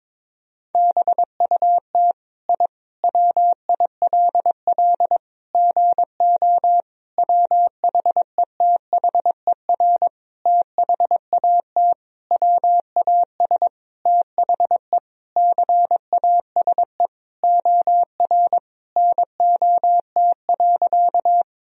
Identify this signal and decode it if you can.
Morse code, heard as a keyed tone, about 22 wpm, 700 Hz. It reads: BUT I WILL GO WHETHER THAT WAS THE CASE OR NOT.
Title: Sentences from Top 1000 Words - Review Part 3 - 22wpm